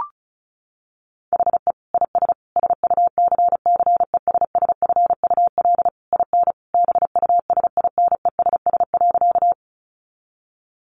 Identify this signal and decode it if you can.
Morse code, heard as a keyed tone, about 34 wpm, 700 Hz. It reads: HE IS SUCCESSFUL IN BUSINESS.
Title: Sentences from Top 900 Words 35wpm